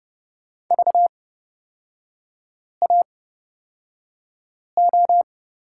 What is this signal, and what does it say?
Morse code, heard as a keyed tone, about 31 wpm, spaced as for 5 wpm, 700 Hz. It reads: VAO
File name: Lesson 38e - Letter q - Three Characters 30wpm